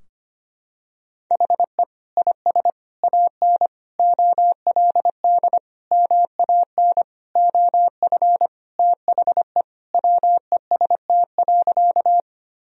Morse code, heard as a keyed tone, about 25 wpm, 700 Hz. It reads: HE IS AN OLD MAN OF THE WEST.
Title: Sentences from Top 1000 Words - Review Part 2 - 25wpm